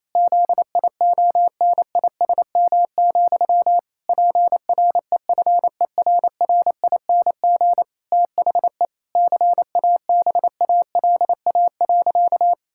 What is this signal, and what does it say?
Morse code, heard as a keyed tone, about 28 wpm, 700 Hz. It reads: ZIONISM, PREFERRING THE CABALA.
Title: Pangram Challenge v2 - 28wpm